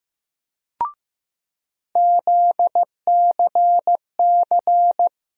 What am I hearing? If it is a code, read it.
Morse code, heard as a keyed tone, about 15 wpm, 700 Hz. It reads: ZCC
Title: USA Call Sign Postfixes - 3 Letters 15wpm